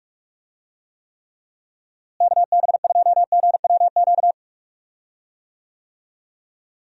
Morse code, heard as a keyed tone, about 43 wpm, 700 Hz. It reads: KB2GWQ
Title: USA Call Signs - 6 Characters - Repeated 3x - 45wpm